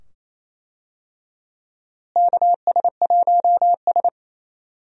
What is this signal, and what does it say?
Morse code, heard as a keyed tone, about 28 wpm, 700 Hz. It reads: KS1S
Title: USA Call Signs - 4 Characters - Repeated 2x - 28wpm